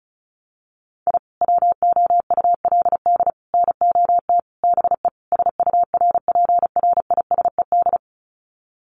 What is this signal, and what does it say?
Morse code, heard as a keyed tone, about 34 wpm, 700 Hz. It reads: I WOULD NOT BE SURPRISED
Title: Sets of 5 Words Encore2 35wpm